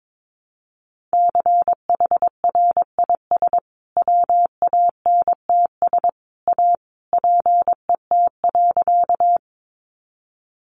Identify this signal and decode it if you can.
Morse code, heard as a keyed tone, about 22 wpm, 700 Hz. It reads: CHRIS WANTS A PET.
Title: Additional Easy Phrases 22wpm